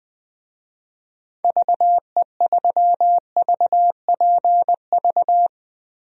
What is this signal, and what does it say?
Morse code, heard as a keyed tone, about 21 wpm, 700 Hz. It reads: VE3VPV
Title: DXCC-100 Call Signs - 6 Characters - Repeated 3x - 20wpm